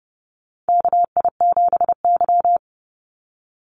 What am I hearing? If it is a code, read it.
Morse code, heard as a keyed tone, about 30 wpm, 700 Hz. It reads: KI7Y
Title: Call Signs Repeated 2x - 30wpm